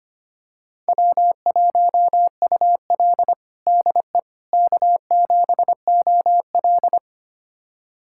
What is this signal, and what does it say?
Morse code, heard as a keyed tone, about 25 wpm, 700 Hz. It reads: W1UL DE K7OL
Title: USA Callsign DE Callsign - 4 Characters - 25wpm